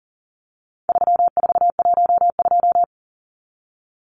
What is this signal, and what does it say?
Morse code, heard as a keyed tone, about 40 wpm, 700 Hz. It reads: 3412
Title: Numbers - 4-Digit 40wpm